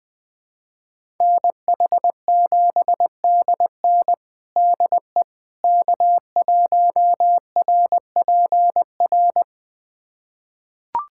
Morse code, heard as a keyed tone, about 20 wpm, 700 Hz. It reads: NH7DN DE K1RPR
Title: USA Callsign DE Callsign 20wpm